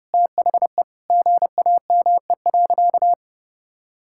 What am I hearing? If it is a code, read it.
Morse code, heard as a keyed tone, about 30 wpm, 700 Hz. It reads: THE GAME.
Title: Sentences from Top 1000 Words - Review Part 6 - 30wpm